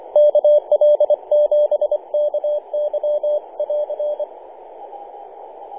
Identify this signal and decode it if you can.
Morse code, heard as a keyed tone, about 25 wpm, 600 Hz. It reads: KL7KY<AR>
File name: kl7ky-50-085mhz-cw